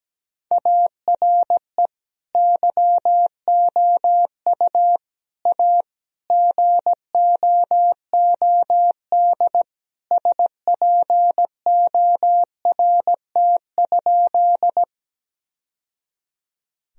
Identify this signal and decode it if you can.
Morse code, heard as a keyed tone, about 17 wpm, 700 Hz. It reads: ARE YOU A GOOD SPORT?
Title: Sentences from Top 1000 Words - Review Part 4 - 17wpm